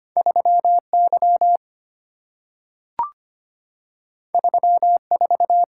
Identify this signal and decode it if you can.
Morse code, heard as a keyed tone, about 25 wpm, 690 Hz. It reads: 3Y 34
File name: Two Letter-Number Combinations - 25wpm